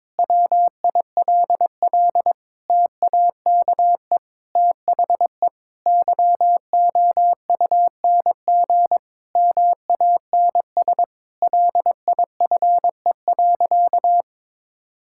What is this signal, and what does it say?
Morse code, heard as a keyed tone, about 22 wpm, 700 Hz. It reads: WILL TAKE THE YOUNG MANS LIFE.
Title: Sentences from Top 1000 Words - Review Part 6 - 22wpm